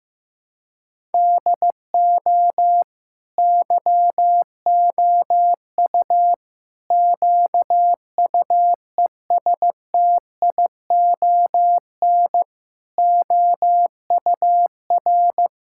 Morse code, heard as a keyed tone, about 14 wpm, 700 Hz. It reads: DO YOU QUESTION OUR
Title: Sentences from Top 1000 Words - Review Part 5 - 15wpm